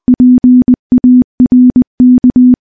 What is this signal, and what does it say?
Morse code, heard as a keyed tone, about 20 wpm, 250 Hz. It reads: PARK
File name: morse / morse-training-20WPM